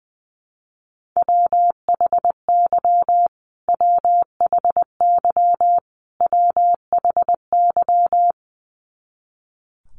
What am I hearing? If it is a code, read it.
Morse code, heard as a keyed tone, about 20 wpm, 700 Hz. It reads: WHY WHY WHY
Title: Top 200 Words Repeated 20wpm